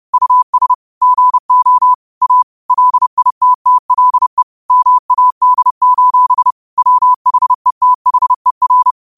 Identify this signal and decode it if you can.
Morse code, heard as a keyed tone, about 30 wpm, 1000 Hz. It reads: US GO A LITTLE MAD: WHETHER
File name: qotdfiles / 30wpm